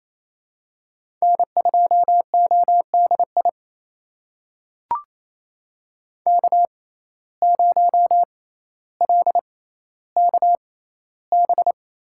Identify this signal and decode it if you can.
Morse code, heard as a keyed tone, about 27 wpm, spaced as for 14 wpm, 700 Hz. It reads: N2ODI K0LKB
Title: USA Call Signs - 5 Characters - 6x Character Spacing - 28wpm